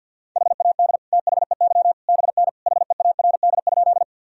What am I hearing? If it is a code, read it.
Morse code, heard as a keyed tone, about 46 wpm, 700 Hz. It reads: HAD THEY BN HEARD?
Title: Sentences from Top 400 Words and Abbreviations 50wpm